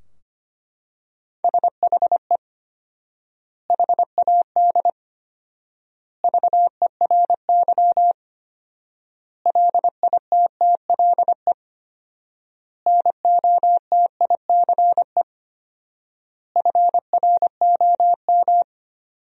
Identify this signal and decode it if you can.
Morse code, heard as a keyed tone, about 25 wpm, 700 Hz. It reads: SHE HAD VERY LITTLE NOTICE FROM
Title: 4x Word-Spacing - Sentences from Top 400 Words - 25wpm